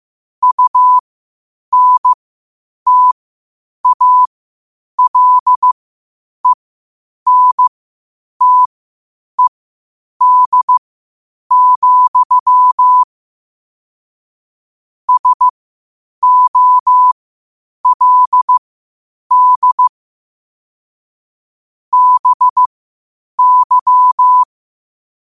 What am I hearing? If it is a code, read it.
Morse code, heard as a keyed tone, about 15 wpm, spaced as for 8 wpm, 1000 Hz. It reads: UNTALENTED, SOLD BY
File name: qotdfiles / 8wpm